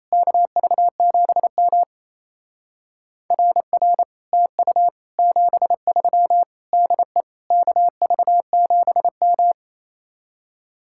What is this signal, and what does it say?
Morse code, heard as a keyed tone, about 34 wpm, 700 Hz. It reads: KV7M RR TU 73 DE KV7M
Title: Speed-Racing Common QSO Phrases Encore - 28wpm